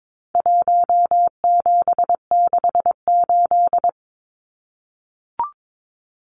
Morse code, heard as a keyed tone, about 22 wpm, 700 Hz. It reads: 1768
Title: Numbers - Contest Style - 4-Digit 22wpm